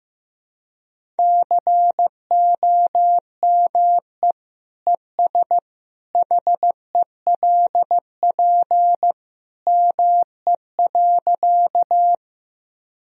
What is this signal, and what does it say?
Morse code, heard as a keyed tone, about 15 wpm, 700 Hz. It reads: COME ES HELP ME.
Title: Sentences from Top 200 Words and Abbreviations 15wpm